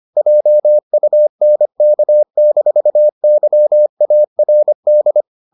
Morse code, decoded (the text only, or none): JUNK-YARD